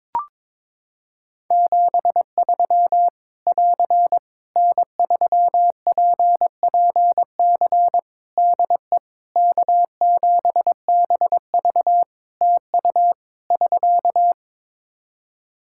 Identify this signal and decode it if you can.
Morse code, heard as a keyed tone, about 22 wpm, 700 Hz. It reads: 73 <AR> N3PPC DE K7BV TU <SK>
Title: Practice QSO - Exchange 4 - Ending - 22wpm